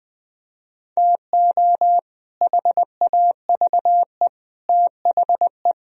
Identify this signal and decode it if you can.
Morse code, heard as a keyed tone, about 21 wpm, 700 Hz. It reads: TO HAVE THE
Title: Speed-Racing - Sets of 3 Words 17wpm